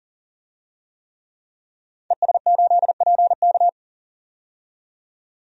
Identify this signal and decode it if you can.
Morse code, heard as a keyed tone, about 40 wpm, 700 Hz. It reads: ES8PK
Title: DXCC-100 Call Signs - 5 Characters - 40wpm